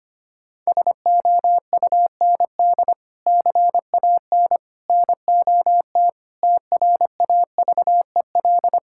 Morse code, heard as a keyed tone, about 25 wpm, 700 Hz. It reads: SOUND CAN NOT TRAVEL
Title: Sentences from Top 1000 Words - Review Part 6 - 25wpm